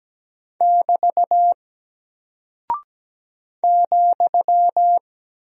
Morse code, heard as a keyed tone, about 17 wpm, 700 Hz. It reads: <BT> ,
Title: Single Character (Letter:Number:Punctuation:Prosign) Alt 17wpm